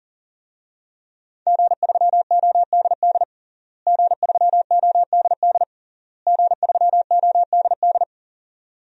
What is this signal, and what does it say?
Morse code, heard as a keyed tone, about 40 wpm, 700 Hz. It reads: G3ODD G3ODD G3ODD
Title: DXCC-100 Call Signs - 5 Characters - Repeated 3x - 40wpm